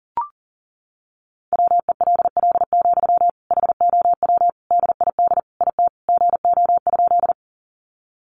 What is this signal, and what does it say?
Morse code, heard as a keyed tone, about 40 wpm, 700 Hz. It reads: WELL, HOW DID IT GO?
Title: Sentences from Top 200 Words 40wpm